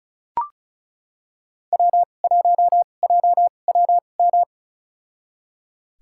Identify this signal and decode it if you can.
Morse code, heard as a keyed tone, about 36 wpm, spaced as for 25 wpm, 700 Hz. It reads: W1JWM
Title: USA Call Signs - 2x Character Spacing - 35wpm